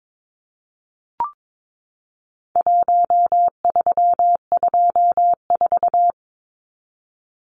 Morse code, heard as a keyed tone, about 22 wpm, 700 Hz. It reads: 1324
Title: Numbers - Contest Style - 4-Digit 22wpm